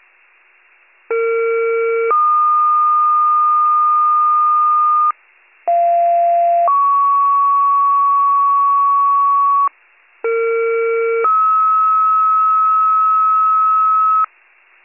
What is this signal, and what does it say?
Two-tone page, heard as a one-second tone, then a three-second tone, about 460 then 1220 Hz, about 690 then 1120 Hz, about 460 then 1290 Hz. It is heard from 1.1 to 14.2 s.